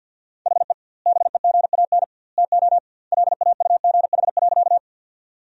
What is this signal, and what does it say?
Morse code, heard as a keyed tone, about 52 wpm, 700 Hz. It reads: HE BEGAN TO LAUGH.
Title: 2x Word-Spacing - Sentences from Top 500 Words - 50wpm